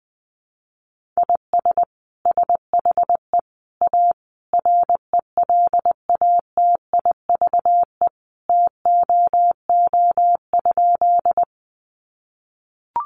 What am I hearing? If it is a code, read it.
Morse code, heard as a keyed tone, about 20 wpm, 700 Hz. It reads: IS SHE A RELATIVE TOO?
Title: Sentences from Top 1400 Words 20wpm